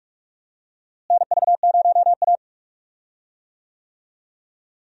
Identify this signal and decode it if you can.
Morse code, heard as a keyed tone, about 45 wpm, 700 Hz. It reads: NV0A
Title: USA Call Signs - 4 Characters - Repeated 2x - 45wpm